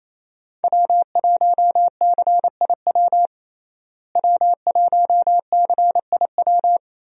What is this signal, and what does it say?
Morse code, heard as a keyed tone, about 28 wpm, 700 Hz. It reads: W1CIW W1CIW
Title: USA Call Signs - 5 Characters - Repeated 2x - 28wpm